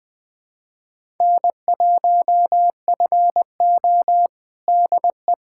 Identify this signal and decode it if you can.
Morse code, heard as a keyed tone, about 20 wpm, 700 Hz. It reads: N1FO DE
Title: USA Callsign DE Callsign - 4 Characters - 20wpm